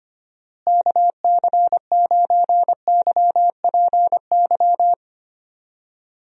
Morse code, heard as a keyed tone, about 25 wpm, 700 Hz. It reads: KC9YPY
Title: USA Call Signs - 6 Characters - Repeated 2x - 25wpm